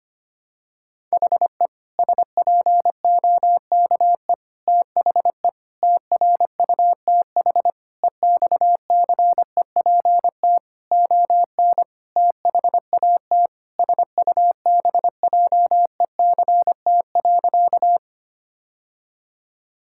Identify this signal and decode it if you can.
Morse code, heard as a keyed tone, about 25 wpm, 700 Hz. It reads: HE SPOKE THE TRUTH EXCEPT ON THAT SUBJECT.
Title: Sentences from Top 1600 Words 25wpm